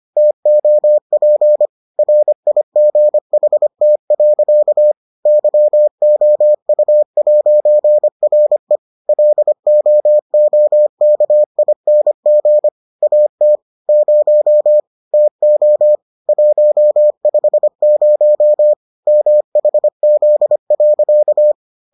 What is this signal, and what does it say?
Morse code, heard as a keyed tone, about 25 wpm, 600 Hz. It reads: TOP RIGHT. YOU'RE LOOKING AT 0 TO 150 MHZ.